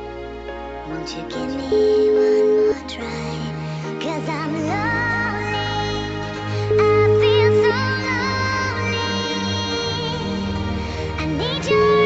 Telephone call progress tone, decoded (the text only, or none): ringback tone